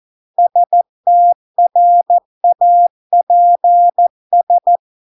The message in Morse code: STRAPS

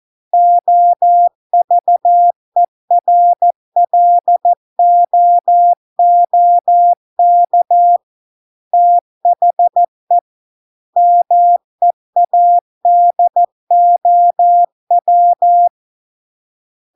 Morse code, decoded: OVERLOOK THE MEADOW

